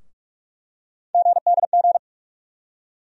Morse code, heard as a keyed tone, about 43 wpm, 700 Hz. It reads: GDG
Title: USA Call Sign Postfixes - 3 Letters 45wpm